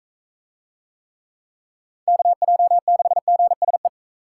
Morse code, heard as a keyed tone, about 38 wpm, 700 Hz. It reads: KJ6GSE